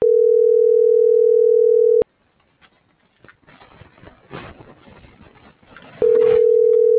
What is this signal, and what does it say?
Ringback tone (North America), 440 + 480 Hz.